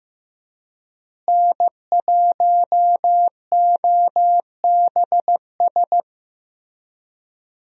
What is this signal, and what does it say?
Morse code, heard as a keyed tone, about 15 wpm, 700 Hz. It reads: N1OBS
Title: USA Call Signs - 5 Characters - Repeated 3x - 15wpm